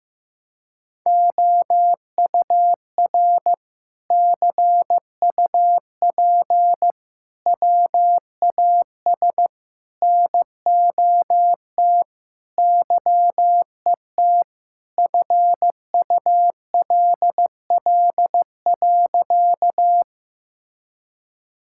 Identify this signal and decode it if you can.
Morse code, heard as a keyed tone, about 14 wpm, 700 Hz. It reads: OUR CUP WAS NOT YET FULL.
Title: Sentences from Top 1600 Words 15wpm